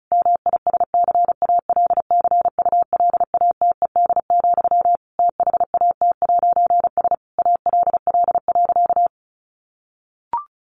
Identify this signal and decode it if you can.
Morse code, heard as a keyed tone, about 34 wpm, 700 Hz. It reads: MISCALCULATED, THAT'S ALL.
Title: A Sound Of Thunder 35wpm